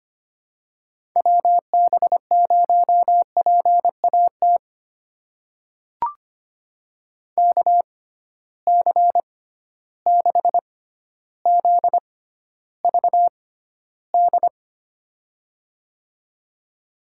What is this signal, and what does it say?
Morse code, heard as a keyed tone, about 25 wpm, spaced as for 13 wpm, 700 Hz. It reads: WB0PAT KC6ZVD
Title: USA Call Signs - 6 Characters - 6x Character Spacing - 25wpm